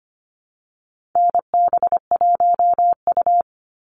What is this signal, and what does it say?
Morse code, heard as a keyed tone, about 25 wpm, 700 Hz. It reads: NB1U
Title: Call Signs Repeated 2x - 25wpm